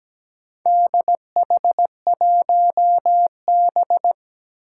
Morse code, heard as a keyed tone, about 17 wpm, 700 Hz. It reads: DH1B
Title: Four Challenging Letter-Number Combinations 17wpm